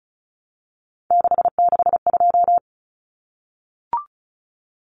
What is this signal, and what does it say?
Morse code, heard as a keyed tone, about 34 wpm, 700 Hz. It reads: 662
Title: Numbers - 3-Digit 35wpm